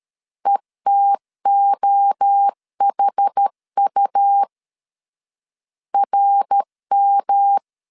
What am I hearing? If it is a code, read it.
Morse code, heard as a keyed tone, about 13 wpm, 800 Hz. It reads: ETOHU RM
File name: CW / DET / 1 3 1500C S A 12wpm 2905